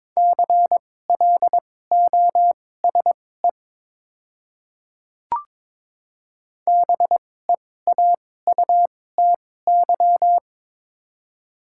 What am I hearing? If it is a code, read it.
Morse code, heard as a keyed tone, about 22 wpm, spaced as for 16 wpm, 700 Hz. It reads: CLOSE BEAUTY
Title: 2x Character Spacing - Top 500 Words - 22wpm